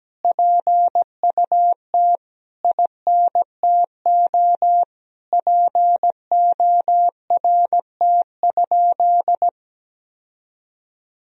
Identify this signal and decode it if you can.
Morse code, heard as a keyed tone, about 17 wpm, 700 Hz. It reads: PUT INTO PORT?
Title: Sentences from Top 1000 Words - Review Part 1 - 17wpm